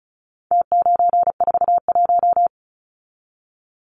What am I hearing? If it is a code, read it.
Morse code, heard as a keyed tone, about 34 wpm, 700 Hz. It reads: T941
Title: Numbers - Contest Style - 4-Digit 35wpm